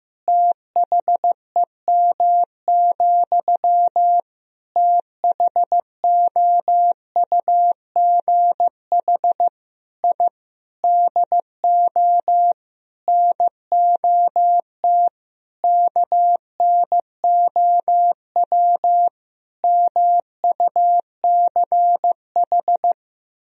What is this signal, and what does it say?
Morse code, heard as a keyed tone, about 15 wpm, 700 Hz. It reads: THEM, THOUGH I DO NOT KNOW MUCH
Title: Sentences from Top 1000 Words - Review Part 2 - 15wpm